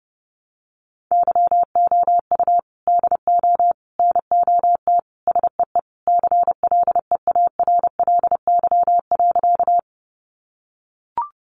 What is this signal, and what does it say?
Morse code, heard as a keyed tone, about 30 wpm, 700 Hz. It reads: YOU DO NOT SEE CLEARLY.
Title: Sentences from Top 1300 Words 30wpm